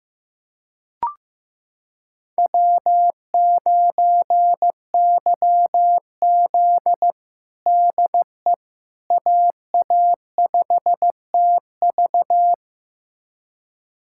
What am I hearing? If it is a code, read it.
Morse code, heard as a keyed tone, about 15 wpm, 700 Hz. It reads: W9YZ DE AA5TV
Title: USA Callsign DE Callsign 15wpm